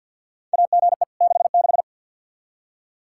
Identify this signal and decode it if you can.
Morse code, heard as a keyed tone, about 47 wpm, 700 Hz. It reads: AGE 66
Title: K1USN SST QSO Practice 50wpm